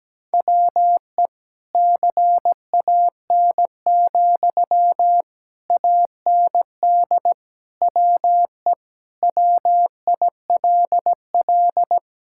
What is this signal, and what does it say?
Morse code, heard as a keyed tone, about 17 wpm, 700 Hz. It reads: WE CAN, AND WE WILL
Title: Sentences from Top 1000 Words - Review Part 6 - 17wpm